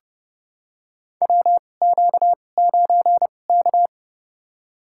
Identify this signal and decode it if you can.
Morse code, heard as a keyed tone, about 30 wpm, spaced as for 22 wpm, 700 Hz. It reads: WQ9K
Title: USA Call Signs - 4 Characters - 2x Character Spacing - 30wpm